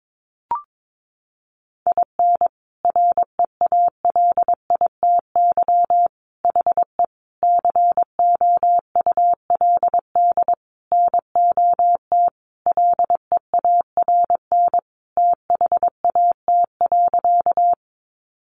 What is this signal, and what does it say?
Morse code, heard as a keyed tone, about 22 wpm, 700 Hz. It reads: IN REALITY HE COULD NOT LEARN THAT.
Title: Sentences from Top 1200 Words 22wpm